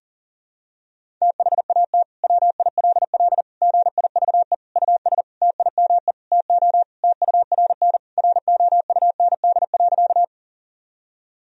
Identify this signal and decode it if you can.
Morse code, heard as a keyed tone, about 37 wpm, 700 Hz. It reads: THAT WILL GIVE US TIME TO TURN ROUND.
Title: Sentences from Top 1000 Words - Review Part 1 - 40wpm